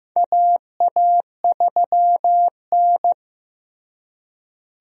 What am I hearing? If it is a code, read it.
Morse code, heard as a keyed tone, about 15 wpm, 700 Hz. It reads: AA3N
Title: USA Call Signs - 4 Characters - Repeated 2x - 15wpm